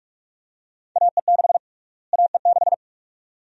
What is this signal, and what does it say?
Morse code, heard as a keyed tone, about 41 wpm, 700 Hz. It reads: AE6 AE6
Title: USA Call Sign Prefixes - Repeated 2x - 45wpm